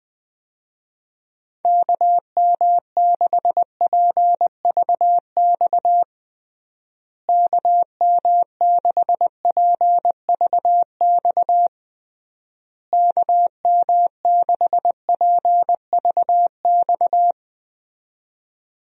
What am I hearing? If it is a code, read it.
Morse code, heard as a keyed tone, about 20 wpm, 700 Hz. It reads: KM6PVX KM6PVX KM6PVX
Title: USA Call Signs - 6 Characters - Repeated 3x - 20wpm